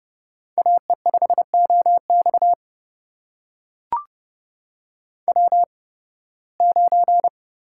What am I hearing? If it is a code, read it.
Morse code, heard as a keyed tone, about 30 wpm, spaced as for 20 wpm, 700 Hz. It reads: AE5OX W9
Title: USA Call Signs - 5 Characters - 8x Character Spacing - 30wpm